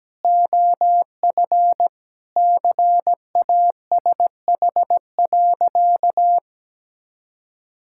Morse code, heard as a keyed tone, about 17 wpm, 700 Hz. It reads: OF CASH.
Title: Sentences from Top 1000 Words - Review Part 5 - 17wpm